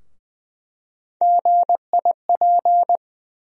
Morse code, heard as a keyed tone, about 20 wpm, 700 Hz. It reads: GIP
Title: USA Call Sign Postfixes - 3 Letters 20wpm